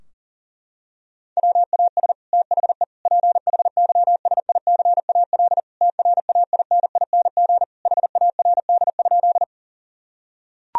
Morse code, heard as a keyed tone, about 40 wpm, 700 Hz. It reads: WAS THE PHYSICAL TRAINING HARD?
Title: Sentences from Top 1000 Words - Review Part 4 - 40wpm